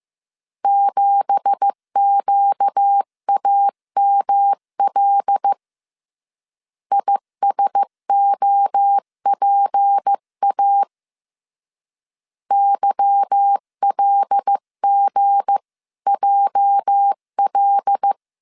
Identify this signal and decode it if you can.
Morse code, heard as a keyed tone, about 15 wpm, 800 Hz. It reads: 7QAML ISOPA YLGJL
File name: CW / DET / 1 9 1500C S B 14wpm 3302